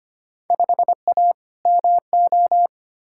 Morse code, heard as a keyed tone, about 25 wpm, 700 Hz. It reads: 5A MO
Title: Field Day Exchanges - 25wpm